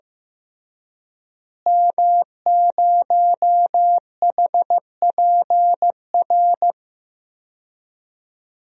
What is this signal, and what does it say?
Morse code, heard as a keyed tone, about 15 wpm, 700 Hz. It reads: M0HPR
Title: DXCC-100 Call Signs - 5 Characters - Repeated 2x - 15wpm